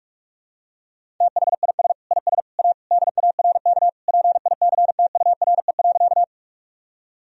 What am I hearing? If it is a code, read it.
Morse code, heard as a keyed tone, about 46 wpm, 700 Hz. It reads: THIS IS A DARK PICTURE.
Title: Sentences from Top 1000 Words - Review Part 1 - 45wpm